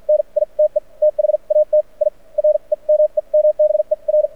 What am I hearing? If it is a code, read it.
Morse code, heard as a keyed tone, about 49 wpm, spaced as for 39 wpm, 600 Hz. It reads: NITE THAT I REMEMBER
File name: archived / CRSnet